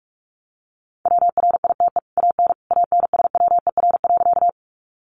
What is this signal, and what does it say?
Morse code, heard as a keyed tone, about 44 wpm, 700 Hz. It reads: WRITE AN ANSWER.
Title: Sentences from Top 300 Words 45wpm